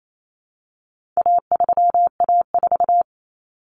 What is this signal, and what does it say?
Morse code, heard as a keyed tone, about 28 wpm, 700 Hz. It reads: A3A4
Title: Common Cut Numbers - 4-Digit 28wpm